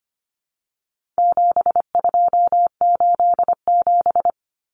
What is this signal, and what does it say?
Morse code, heard as a keyed tone, about 25 wpm, 700 Hz. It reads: 7287
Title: Common Cut Numbers - 4-Digit 25wpm